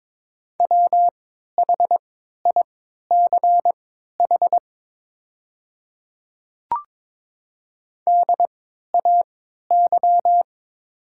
Morse code, heard as a keyed tone, about 22 wpm, spaced as for 12 wpm, 700 Hz. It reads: WHICH DAY
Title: 3x Character Spacing - Top 100 Words - 22wpm